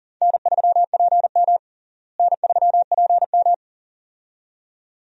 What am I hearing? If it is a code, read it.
Morse code, heard as a keyed tone, about 40 wpm, 700 Hz. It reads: N3PM N3PM
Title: USA Call Signs - 4 Characters - Repeated 2x - 40wpm